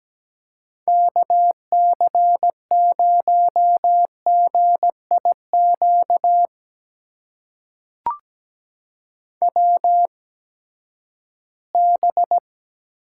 Morse code, heard as a keyed tone, about 17 wpm, spaced as for 12 wpm, 700 Hz. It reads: KC0GIQ WB
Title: USA Call Signs - 8x Character Spacing - 17wpm